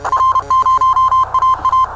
Morse code, heard as a keyed tone, about 30 wpm, 1040 Hz. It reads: R0AA
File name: attachments / R0AA